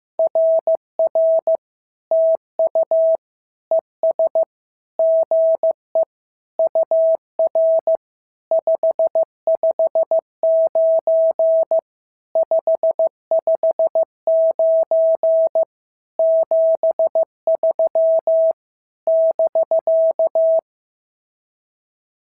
Morse code, heard as a keyed tone, about 15 wpm, 650 Hz. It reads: RR TU ES GE UR 559 559 73 <BK>
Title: SOTA QSO Practice 15wpm